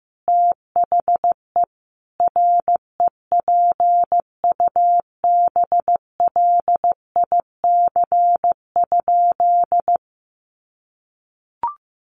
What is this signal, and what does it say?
Morse code, heard as a keyed tone, about 15 wpm, 700 Hz. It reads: THE REPUBLIC?
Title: Sentences from Top 1500 Words 15wpm